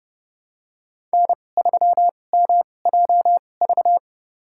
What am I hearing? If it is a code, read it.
Morse code, heard as a keyed tone, about 30 wpm, spaced as for 22 wpm, 700 Hz. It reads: N3MJV
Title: USA Call Signs - 2x Character Spacing - 30wpm